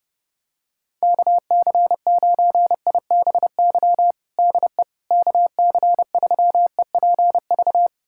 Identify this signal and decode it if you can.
Morse code, heard as a keyed tone, about 31 wpm, 700 Hz. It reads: KC9IBY DE KC3EPV
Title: USA Callsign DE Callsign - 6 Characters - 30wpm